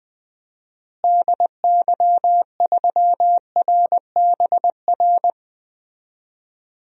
Morse code, heard as a keyed tone, about 20 wpm, 700 Hz. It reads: DY3RBR
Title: DXCC-100 Call Signs - 6 Characters - Repeated 2x - 20wpm